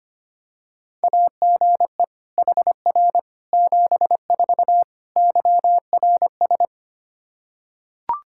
Morse code, heard as a keyed tone, about 25 wpm, 700 Hz. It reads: AGE HR 74 YRS
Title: Practice QSO - Exchange 3 - More Info - 25wpm